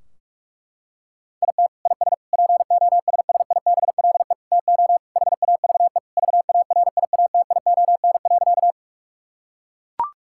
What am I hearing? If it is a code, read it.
Morse code, heard as a keyed tone, about 41 wpm, 700 Hz. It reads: IT IS POSSIBLE TO HAVE VARIATION.
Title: Sentences from Top 1600 Words 45wpm